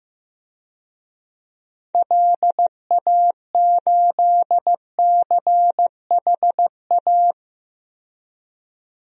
Morse code, heard as a keyed tone, about 15 wpm, 700 Hz. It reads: LA8CHA
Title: DXCC-100 Call Signs - 6 Characters - 15wpm